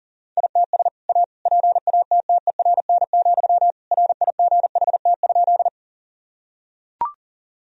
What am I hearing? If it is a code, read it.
Morse code, heard as a keyed tone, about 40 wpm, 700 Hz. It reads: ITS A PATTERN, RIGHT?
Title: Sentences from Top 1000 Words - Review Part 2 - 40wpm